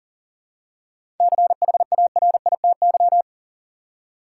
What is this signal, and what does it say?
Morse code, heard as a keyed tone, about 37 wpm, 700 Hz. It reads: CHARITY
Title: Seven Letter Words - 40wpm